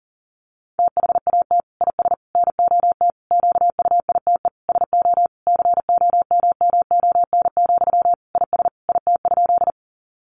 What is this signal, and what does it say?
Morse code, heard as a keyed tone, about 40 wpm, 700 Hz. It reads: THAT IS NOT QUITE SO COMMON, IS IT?
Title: Sentences from Top 700 Words 40wpm